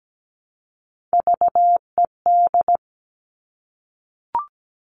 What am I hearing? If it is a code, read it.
Morse code, heard as a keyed tone, about 17 wpm, 700 Hz. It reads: VED
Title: All Cut Numbers - 3-Digit 17wpm